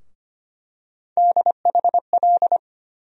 Morse code, heard as a keyed tone, about 25 wpm, 700 Hz. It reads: DHL
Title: USA Call Sign Postfixes - 3 Letters 25wpm